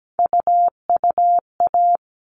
Morse code, heard as a keyed tone, about 17 wpm, 700 Hz. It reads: UUA
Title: All Cut Numbers - 3-Digit 17wpm